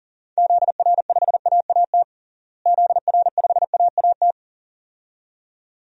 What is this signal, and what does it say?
Morse code, heard as a keyed tone, about 40 wpm, 700 Hz. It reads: ZR5AAT ZR5AAT
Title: DXCC-100 Call Signs - 6 Characters - Repeated 2x - 40wpm